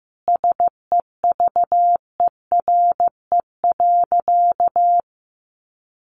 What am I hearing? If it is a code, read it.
Morse code, heard as a keyed tone, about 15 wpm, 700 Hz. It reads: SEVERE.
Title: Sentences from Top 1400 Words 15wpm